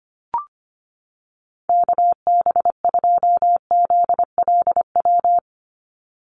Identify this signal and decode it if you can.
Morse code, heard as a keyed tone, about 25 wpm, 700 Hz. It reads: KB2ZLW
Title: USA Call Signs 25wpm